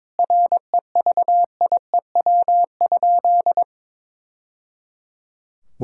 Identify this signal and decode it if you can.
Morse code, heard as a keyed tone, about 21 wpm, 700 Hz. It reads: REVIEW?
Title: Sentences from Top 1000 Words - Review Part 6 - 22wpm